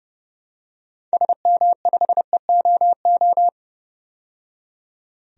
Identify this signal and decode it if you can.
Morse code, heard as a keyed tone, about 30 wpm, 700 Hz. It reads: SM5EOO
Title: DXCC-100 Call Signs - 6 Characters - 30wpm